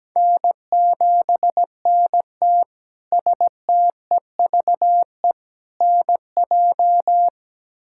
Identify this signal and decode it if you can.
Morse code, heard as a keyed tone, about 17 wpm, 700 Hz. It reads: N7NT STEVE NJ
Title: K1USN SST QSO Practice - 17wpm